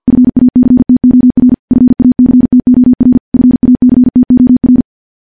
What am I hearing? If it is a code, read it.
Morse code, heard as a keyed tone, about 50 wpm, 250 Hz. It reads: FACTOR FACTOR FACTOR
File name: morse / morse-training-50WPM